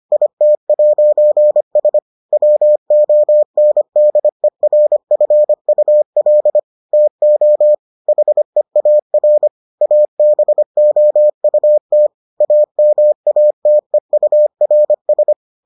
Morse code, heard as a keyed tone, about 25 wpm, 600 Hz. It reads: IT'S WONDERFUL TO HEAR ABOUT AMATEURS